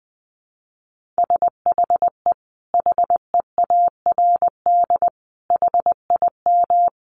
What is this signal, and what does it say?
Morse code, heard as a keyed tone, about 20 wpm, 700 Hz. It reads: SHE HEARD HIM
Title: Sentences from Top 1400 Words 20wpm